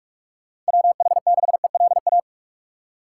Morse code, heard as a keyed tone, about 47 wpm, 700 Hz. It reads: WH6ELA
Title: USA Call Signs - 6 Characters - 3x Character Spacing - 45wpm